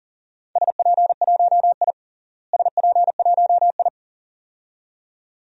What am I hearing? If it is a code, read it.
Morse code, heard as a keyed tone, about 40 wpm, 700 Hz. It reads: SP1I SP1I
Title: DXCC-100 Call Signs - 4 Characters - Repeated 2x - 40wpm